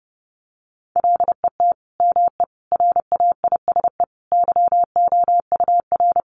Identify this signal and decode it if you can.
Morse code, heard as a keyed tone, about 30 wpm, 700 Hz. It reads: LET ME RAISE YOUR
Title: Sentences from Top 1400 Words 30wpm